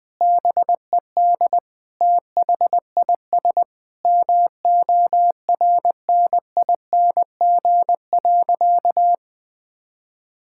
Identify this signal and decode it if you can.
Morse code, heard as a keyed tone, about 19 wpm, 700 Hz. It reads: BED THIS MORNING.
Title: Sentences from Top 1000 Words - Review Part 6 - 20wpm